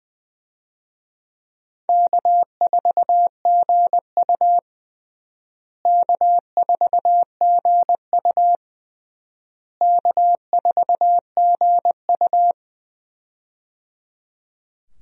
Morse code, heard as a keyed tone, about 20 wpm, 700 Hz. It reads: K4GU K4GU K4GU
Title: USA Call Signs - 4 Characters - Repeated 3x - 20wpm